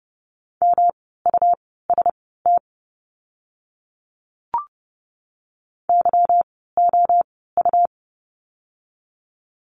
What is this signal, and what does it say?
Morse code, heard as a keyed tone, about 30 wpm, spaced as for 17 wpm, 700 Hz. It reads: MUST YOU
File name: course / 30:10wpm / 237 Lesson 38f - Letter q - Words 30wpm